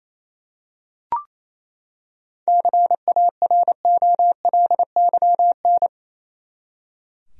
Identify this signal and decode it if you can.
Morse code, heard as a keyed tone, about 28 wpm, 700 Hz. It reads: CAROLYN
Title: USA Common Names - 28wpm